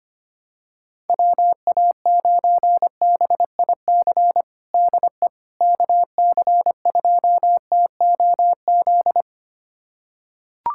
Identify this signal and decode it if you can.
Morse code, heard as a keyed tone, about 25 wpm, 700 Hz. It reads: WA9BIC DE KC2TOZ
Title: USA Callsign DE Callsign 25wpm